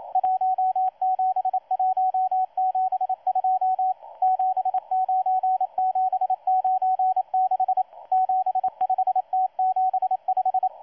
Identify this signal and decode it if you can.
Morse code, heard as a keyed tone, about 27 wpm, 750 Hz, with radio noise. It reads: 17172 79796 75T75
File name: M12-2016-07-07-1310utc-13926khz-error-restart